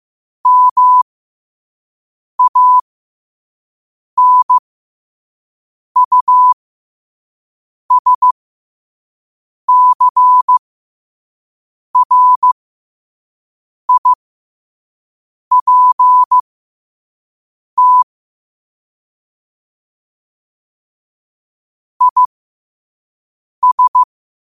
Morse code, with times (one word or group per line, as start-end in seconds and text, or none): 0.5-18.0 MANUSCRIPT
22.0-24.0 IS